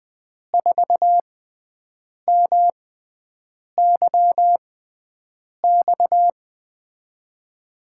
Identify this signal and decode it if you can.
Morse code, heard as a keyed tone, about 20 wpm, spaced as for 7 wpm, 700 Hz. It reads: 4MYX
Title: USA Call Signs - 6 Characters - 6x Character Spacing - 20wpm